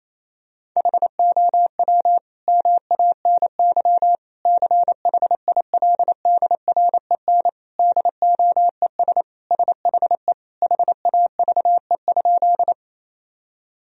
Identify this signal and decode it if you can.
Morse code, heard as a keyed tone, about 28 wpm, 700 Hz. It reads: HOW MANY CHILDREN DOES SHE HAVE?
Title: Sentences from Top 1000 Words - Review Part 1 - 28wpm